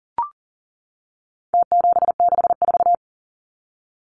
Morse code, heard as a keyed tone, about 40 wpm, 700 Hz. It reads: T764
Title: Numbers - Contest Style - 4-Digit 40wpm